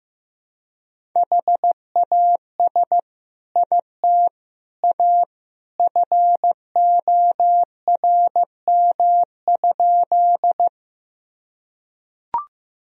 Morse code, heard as a keyed tone, about 15 wpm, 700 Hz. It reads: HAS IT A FORM?
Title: Sentences from Top 1000 Words - Review Part 1 - 15wpm